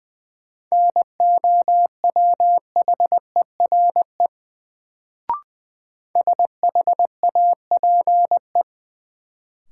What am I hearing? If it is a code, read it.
Morse code, heard as a keyed tone, about 20 wpm, 700 Hz. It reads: NOWHERE SHAPE
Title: Sets of 5 Words Encore - New Word Drill - 20wpm